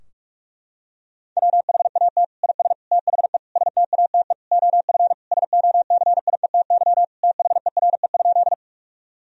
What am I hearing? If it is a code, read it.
Morse code, heard as a keyed tone, about 46 wpm, 700 Hz. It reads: WHAT IS THE STATE OF SOCIETY THERE?
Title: Sentences from Top 1000 Words - Review Part 4 - 45wpm